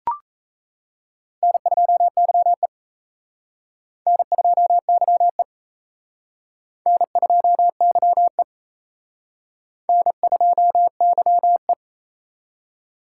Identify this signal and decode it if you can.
Morse code, heard as a keyed tone, about 34 wpm, 700 Hz. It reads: N2YE N2YE N2YE N2YE
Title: Speed Racing - USA Call Signs 28wpm